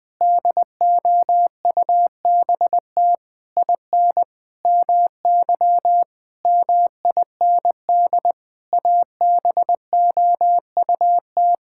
Morse code, heard as a keyed tone, about 20 wpm, 700 Hz. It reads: DOUBT IN MY MIND ABOUT
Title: Sentences with 7-Word Phrases - 20wpm